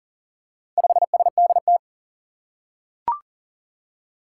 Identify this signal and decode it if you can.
Morse code, heard as a keyed tone, about 40 wpm, 690 Hz. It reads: 5SDT
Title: Four Challenging Letter-Number Combinations 40wpm